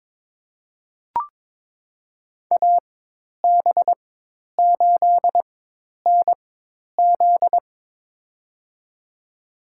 Morse code, heard as a keyed tone, about 22 wpm, spaced as for 10 wpm, 700 Hz. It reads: AB8NZ
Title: USA Call Signs - 5 Characters - 4x Character Spacing - 22wpm